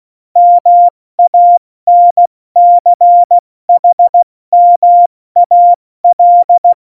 Morse code, heard as a keyed tone, about 16 wpm, 700 Hz. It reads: MANCHMAL
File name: Morsekurs / Witze / Witz 03.Dezember 2019, 70 F80